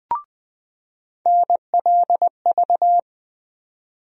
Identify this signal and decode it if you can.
Morse code, heard as a keyed tone, about 20 wpm, 700 Hz. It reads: NLV
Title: USA Call Sign Postfixes - 3 Letters 20wpm